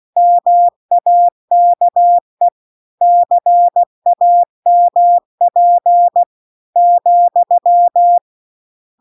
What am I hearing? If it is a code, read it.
Morse code, heard as a keyed tone, about 16 wpm, 700 Hz. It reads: MAKE CAMP ,